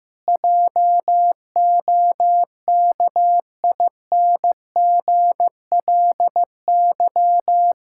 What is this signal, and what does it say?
Morse code, heard as a keyed tone, about 15 wpm, 700 Hz. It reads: JOKINGLY
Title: Pangram Challenge v2 - 15wpm